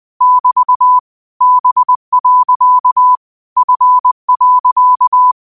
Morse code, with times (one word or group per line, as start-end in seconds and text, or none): 0.2-1.0 <BT>
1.4-3.2 B.
3.6-5.3 F.